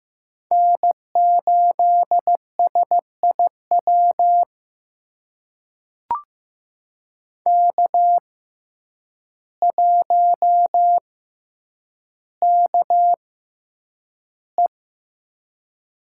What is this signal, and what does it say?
Morse code, heard as a keyed tone, about 15 wpm, spaced as for 8 wpm, 700 Hz. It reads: N8SIW K1KE
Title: USA Call Signs - 5 Characters - 6x Character Spacing - 15wpm